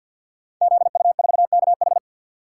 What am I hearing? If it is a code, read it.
Morse code, heard as a keyed tone, about 49 wpm, 700 Hz. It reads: ZU4KH
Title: Code Groups - 5 Letter:Numbers - 50wpm